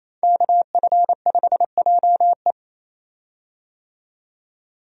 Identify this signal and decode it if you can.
Morse code, heard as a keyed tone, about 28 wpm, 700 Hz. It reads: KF5JE
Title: USA Call Signs - 5 Characters - 28wpm